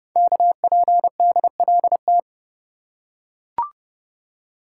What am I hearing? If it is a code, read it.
Morse code, heard as a keyed tone, about 30 wpm, 700 Hz. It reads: KPDLT
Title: Code Groups - 5 Letters - 30wpm